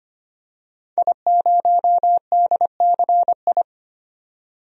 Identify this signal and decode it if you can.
Morse code, heard as a keyed tone, about 25 wpm, 700 Hz. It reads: I0DCI
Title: DXCC-100 Call Signs - 5 Characters - 25wpm